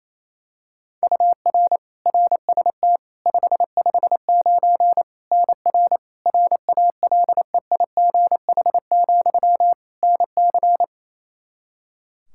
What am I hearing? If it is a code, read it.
Morse code, heard as a keyed tone, about 28 wpm, 700 Hz. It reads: UR RST 559 NR RALEIGH, NC
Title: Practice QSO - Exchange 1 - Beginning - 28wpm